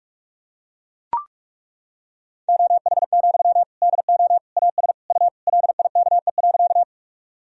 Morse code, heard as a keyed tone, about 41 wpm, 690 Hz. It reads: OH, DO AS U LIKE.
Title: Sentences from Top 500 Words and Abbreviations 45wpm